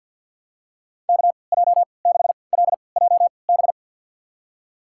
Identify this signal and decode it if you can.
Morse code, heard as a keyed tone, about 52 wpm, spaced as for 29 wpm, 700 Hz. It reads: KJ6LJB
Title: USA Call Signs - 6 Characters - 3x Character Spacing - 50wpm